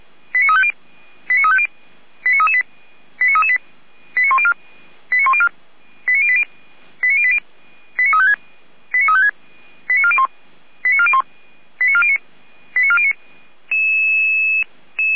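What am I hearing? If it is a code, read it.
Five-tone selective call (ZVEI), heard as quick runs of five tones, tones of about 70 ms. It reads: 89350 89350 89338 89338 89104 89104 89080 89356 89356 89401 89401 89409 89409
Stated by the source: ZVEI five-tone call received by SDR, logged by the Orlamünde fire brigade (feuerwehr-orlamuende).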